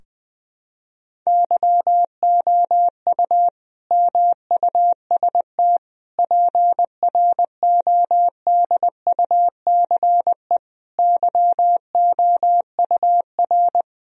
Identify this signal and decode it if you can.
Morse code, heard as a keyed tone, about 20 wpm, 700 Hz. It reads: YOU MUST PRODUCE YOUR
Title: Sentences from Top 1000 Words - Review Part 3 - 20wpm